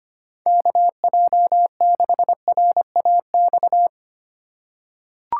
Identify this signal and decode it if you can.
Morse code, heard as a keyed tone, about 25 wpm, 700 Hz. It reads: KJ6RAX
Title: USA Call Signs - 2x Character Spacing - 25wpm